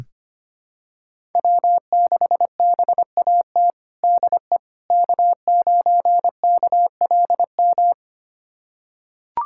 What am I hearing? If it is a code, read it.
Morse code, heard as a keyed tone, about 25 wpm, 700 Hz. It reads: W6BAT DE K9KLM
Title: USA Callsign DE Callsign - 5 Characters - 25wpm